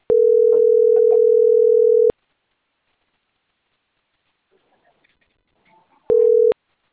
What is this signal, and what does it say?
Ringback tone (North America), 440 + 480 Hz.